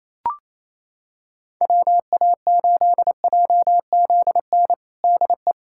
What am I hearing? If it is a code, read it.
Morse code, heard as a keyed tone, about 28 wpm, 700 Hz. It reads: WA8JZN DE
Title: USA Callsign DE Callsign - 6 Characters - 28wpm